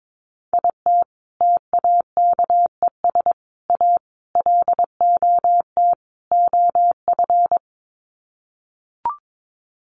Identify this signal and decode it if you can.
Morse code, heard as a keyed tone, about 22 wpm, 700 Hz. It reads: IT TAKES A LOT OF
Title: Sets of 5 Words Encore 22wpm